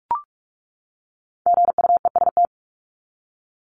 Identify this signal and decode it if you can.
Morse code, heard as a keyed tone, about 44 wpm, 700 Hz. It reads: GUEST
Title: Top 1501-1750 Words 45wpm